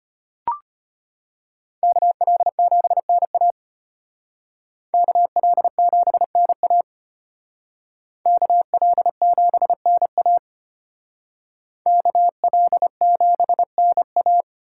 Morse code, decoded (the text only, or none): KL7NA KL7NA KL7NA KL7NA